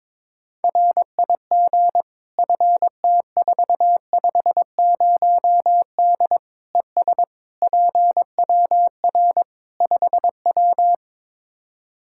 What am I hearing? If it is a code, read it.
Morse code, heard as a keyed tone, about 21 wpm, 700 Hz. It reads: RIG FT450D ES PWR 5W
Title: Practice QSO - Exchange 2 - Info - 22wpm